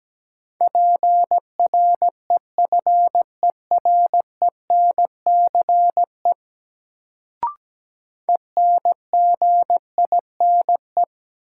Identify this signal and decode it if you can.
Morse code, heard as a keyed tone, about 17 wpm, 700 Hz. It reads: PREFERENCE ENGINE
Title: Sentences from Top 1500 Words - New Word Drill Alt - 17wpm